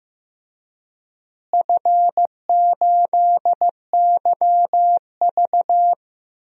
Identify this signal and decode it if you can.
Morse code, heard as a keyed tone, about 15 wpm, 700 Hz. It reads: F8YV
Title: DXCC-100 Call Signs - 4 Characters - Repeated 3x - 15wpm